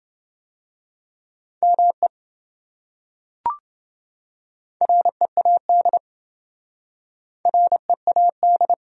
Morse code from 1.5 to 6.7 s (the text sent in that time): ME READ